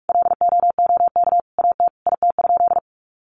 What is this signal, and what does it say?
Morse code, heard as a keyed tone, about 44 wpm, 700 Hz. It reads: LOOK AT IT?
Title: Sentences from Top 100 Words 45wpm